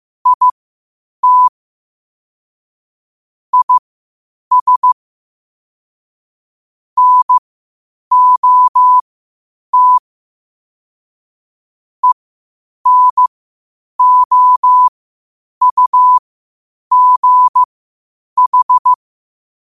Morse code, heard as a keyed tone, about 15 wpm, spaced as for 8 wpm, 1000 Hz. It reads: IT IS NOT ENOUGH
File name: qotdfiles / 8wpm